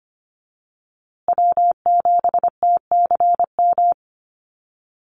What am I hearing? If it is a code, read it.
Morse code, heard as a keyed tone, about 25 wpm, 700 Hz. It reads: W7TCM
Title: USA Call Signs Repeated 2x 25wpm